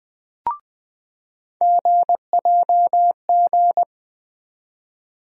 Morse code, heard as a keyed tone, about 20 wpm, 700 Hz. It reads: GJG
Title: USA Call Sign Postfixes - 3 Letters 20wpm